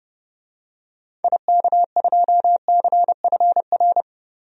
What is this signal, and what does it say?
Morse code, heard as a keyed tone, about 30 wpm, 700 Hz. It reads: IK2CFR
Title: DXCC-100 Call Signs - 6 Characters - Repeated 2x - 30wpm